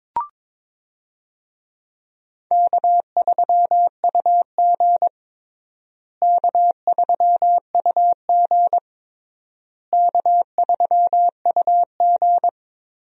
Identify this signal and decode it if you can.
Morse code, heard as a keyed tone, about 21 wpm, 700 Hz. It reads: K3UG K3UG K3UG
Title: USA Call Signs - 4 Characters - Repeated 3x - 22wpm